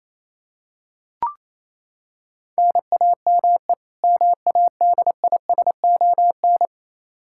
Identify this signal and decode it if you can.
Morse code, heard as a keyed tone, about 28 wpm, 700 Hz. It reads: NAME MADISON
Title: Practice QSO - Exchange 1 - Beginning - 28wpm